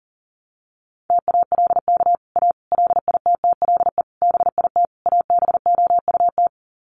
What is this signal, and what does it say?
Morse code, heard as a keyed tone, about 40 wpm, 700 Hz. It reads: TALK A LITTLE BIT ABOUT
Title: Sets of 5 Words Encore 40wpm